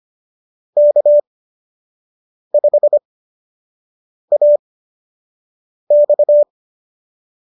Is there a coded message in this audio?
Yes: Morse code from 0.8 to 6.4 s, heard as a keyed tone, about 25 wpm, spaced as for 6 wpm, 590 Hz. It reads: K5AX